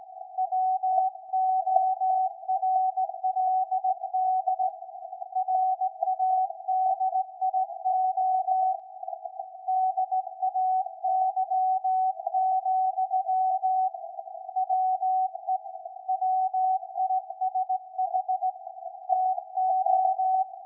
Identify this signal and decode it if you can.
Morse code, heard as a keyed tone, about 18 wpm, 750 Hz. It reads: WORLD RADIO DAY, WE WISH TO
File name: iz7slz / VLF / SAQ 13-02-2023